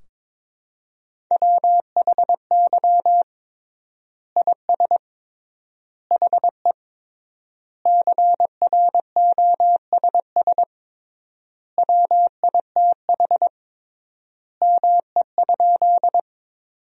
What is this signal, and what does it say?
Morse code, heard as a keyed tone, about 22 wpm, 700 Hz. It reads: WHY IS HE CROSS WITH ME?
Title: 3x Word-Spacing - Sentences from Top 300 Words - 22wpm